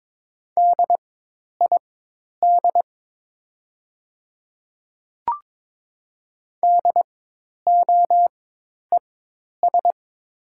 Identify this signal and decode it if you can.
Morse code, heard as a keyed tone, about 22 wpm, spaced as for 10 wpm, 700 Hz. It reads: DID DOES
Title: 4x Character Spacing - Top 500 Words - 22wpm